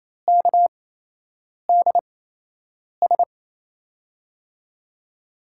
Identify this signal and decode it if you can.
Morse code, heard as a keyed tone, about 28 wpm, spaced as for 8 wpm, 700 Hz. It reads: KDS
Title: USA Call Signs - 6 Characters - 8x Character Spacing - 28wpm